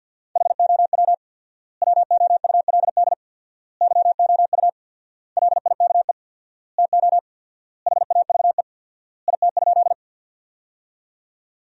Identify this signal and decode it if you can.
Morse code, heard as a keyed tone, about 52 wpm, 700 Hz. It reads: HOW WOULD YOU LIKE TO HAVE IT?
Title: 4x Word-Spacing - Sentences from Top 100 Words - 50wpm